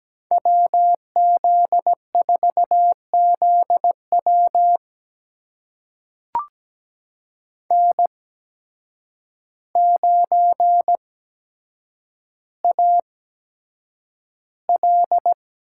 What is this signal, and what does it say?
Morse code, heard as a keyed tone, about 17 wpm, spaced as for 8 wpm, 700 Hz. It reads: WZ4ZW N9AL
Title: USA Call Signs - 5 Characters - 8x Character Spacing - 17wpm